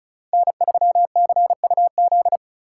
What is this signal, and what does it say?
Morse code, heard as a keyed tone, about 45 wpm, 700 Hz. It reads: N3CUZ